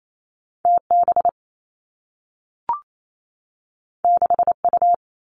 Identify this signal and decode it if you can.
Morse code, heard as a keyed tone, about 28 wpm, 700 Hz. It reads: T B 6 U
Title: Two Challenging Letter-Number Combinations 28wpm